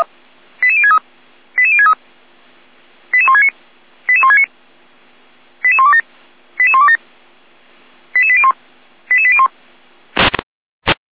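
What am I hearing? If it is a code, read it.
Five-tone selective call (ZVEI), heard as quick runs of five tones, tones of about 70 ms. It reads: 88063 88063 88169 88169 88127 88127 88801 88801
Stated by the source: ZVEI five-tone selective call received on an SDR.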